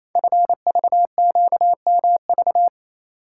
Morse code, heard as a keyed tone, about 27 wpm, 690 Hz. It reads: FVQMV
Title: Code Groups - 5 Letter:Numbers - 28wpm